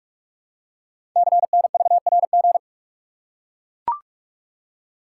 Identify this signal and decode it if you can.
Morse code, heard as a keyed tone, about 44 wpm, 700 Hz. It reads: CNVRG
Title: Code Groups - 5 Letters - 45wpm